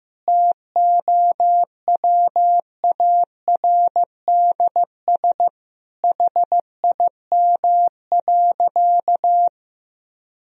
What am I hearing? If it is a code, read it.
Morse code, heard as a keyed tone, about 15 wpm, 700 Hz. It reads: TOWARDS HIM.
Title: Sentences from Top 1000 Words - Review Part 2 - 15wpm